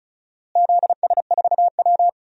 Morse code, heard as a keyed tone, about 34 wpm, 700 Hz. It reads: ZS4W
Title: DXCC-100 Call Signs - 4 Characters - Repeated 3x - 35wpm